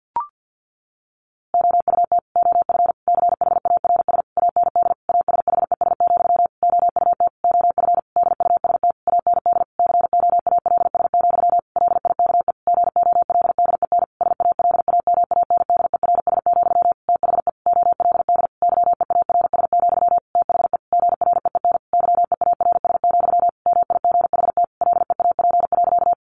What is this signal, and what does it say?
Morse code, heard as a keyed tone, about 50 wpm, 700 Hz. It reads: OUT OF CHARS AND ASHES, OUT OF DUST AND COALS, LIKE GOLDEN SALAMANDERS, THE OLD YEARS, THE GREEN YEARS, MIGHT LEAP.
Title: A Sound Of Thunder 50wpm